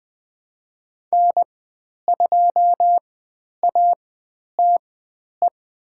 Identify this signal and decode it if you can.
Morse code, heard as a keyed tone, about 20 wpm, spaced as for 10 wpm, 700 Hz. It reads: N2ATE
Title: 095 Lesson 16g - Letter c - Call Signs 20wpm